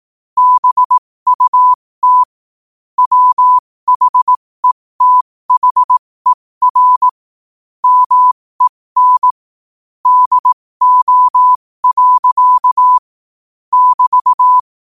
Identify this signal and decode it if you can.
Morse code, heard as a keyed tone, about 18 wpm, spaced as for 15 wpm, 1000 Hz. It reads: BUT WHETHER MEN DO. <BT>
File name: qotdfiles / 15wpm